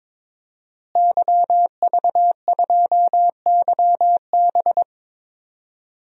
Morse code, heard as a keyed tone, about 22 wpm, 700 Hz. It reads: YV2YB
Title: DXCC-100 Call Signs - 5 Characters - Repeated 3x - 22wpm